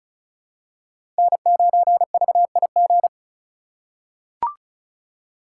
Morse code, heard as a keyed tone, about 36 wpm, 700 Hz. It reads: N9VIG
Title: USA Call Signs - 3x Character Spacing - 35wpm